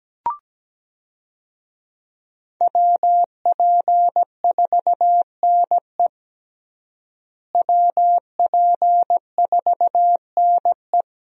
Morse code, heard as a keyed tone, about 17 wpm, 700 Hz. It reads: WP4NE WP4NE
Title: USA Call Signs - 5 Characters - Repeated 2x - 17wpm